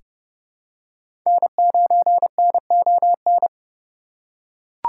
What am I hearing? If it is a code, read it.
Morse code, heard as a keyed tone, about 30 wpm, 700 Hz. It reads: N9NON
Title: USA Call Signs - 3x Character Spacing - 30wpm